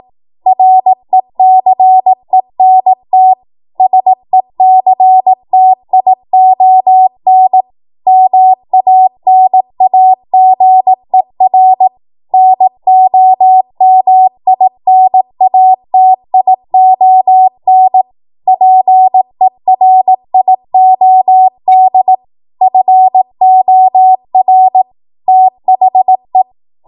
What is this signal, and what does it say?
Morse code, heard as a keyed tone, about 18 wpm, 750 Hz. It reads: RECENT SECTION MANAGER NOMINATION PERIOD FOR THE
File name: Morse / 18WPM